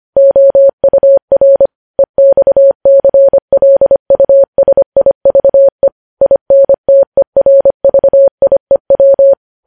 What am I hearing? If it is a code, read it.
Morse code, heard as a keyed tone, about 25 wpm, 550 Hz. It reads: OUR EXCLUSIVE INTERVIEW
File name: podcast / News headlines-1x-25x25